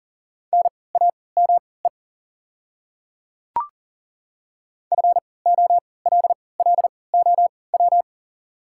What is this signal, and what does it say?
Morse code, heard as a keyed tone, about 40 wpm, spaced as for 23 wpm, 700 Hz. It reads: NAME FOLLOW
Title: 3x Character Spacing - Top 200 Words - 40wpm